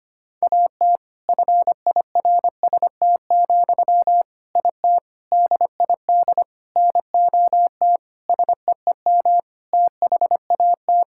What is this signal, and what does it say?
Morse code, heard as a keyed tone, about 25 wpm, 700 Hz. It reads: AT FIRST, IT DID NOT SEEM THAT
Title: Sentences from Top 1000 Words - Review Part 6 - 25wpm